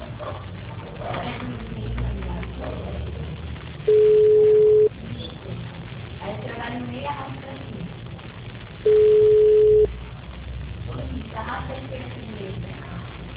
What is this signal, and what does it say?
Ringback tone (Europe, 425 Hz), with speech.